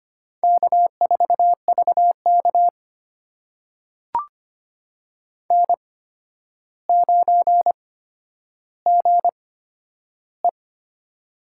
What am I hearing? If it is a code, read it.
Morse code, heard as a keyed tone, about 25 wpm, spaced as for 11 wpm, 700 Hz. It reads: K4VK N9GE
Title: USA Call Signs - 4 Characters - 8x Character Spacing - 25wpm